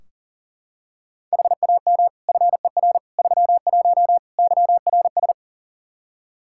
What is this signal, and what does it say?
Morse code, heard as a keyed tone, about 37 wpm, 700 Hz. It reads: HAM FER 31 YRS
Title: Practice QSO - Exchange 3 - More Info - 40wpm